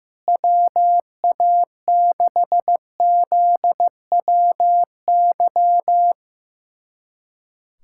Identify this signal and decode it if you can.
Morse code, heard as a keyed tone, about 15 wpm, 700 Hz. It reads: WA6ZWY
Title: USA Callsign DE Callsign - 6 Characters - 15wpm